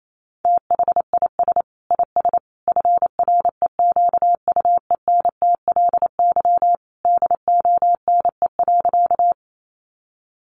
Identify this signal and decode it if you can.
Morse code, heard as a keyed tone, about 28 wpm, 700 Hz. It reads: THIS IS FREQUENTLY DONE.
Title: Sentences from Top 1300 Words 28wpm